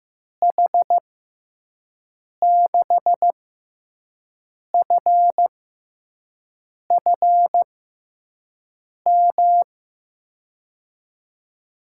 Morse code, heard as a keyed tone, about 15 wpm, spaced as for 5 wpm, 700 Hz. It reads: H6FFM
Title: USA Call Signs - 6x Character Spacing - 15wpm